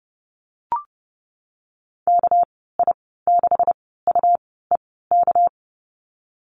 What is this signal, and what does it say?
Morse code, heard as a keyed tone, about 30 wpm, spaced as for 17 wpm, 700 Hz. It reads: KI6UEK
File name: Lesson 37g - Letter x - Call Signs 30wpm